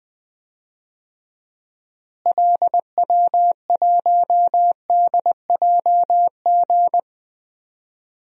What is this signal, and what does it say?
Morse code, heard as a keyed tone, about 20 wpm, 690 Hz. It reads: LW1DJG
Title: DXCC-100 Call Signs - 6 Characters - Repeated 2x - 20wpm